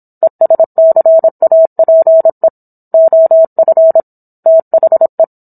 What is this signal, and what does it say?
Morse code, heard as a keyed tone, about 26 wpm, 650 Hz. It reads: ESCAPE OF THE